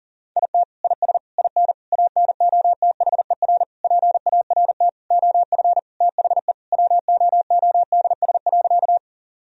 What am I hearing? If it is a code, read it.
Morse code, heard as a keyed tone, about 40 wpm, 700 Hz. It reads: IT IS IN ANOTHER PART OF THE WOODS.
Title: Sentences from Top 1000 Words - Review Part 2 - 40wpm